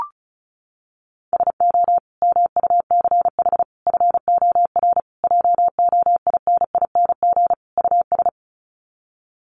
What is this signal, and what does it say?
Morse code, heard as a keyed tone, about 34 wpm, 700 Hz. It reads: SO MUCH FOR JOINING US
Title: Sets of 5 Words 35wpm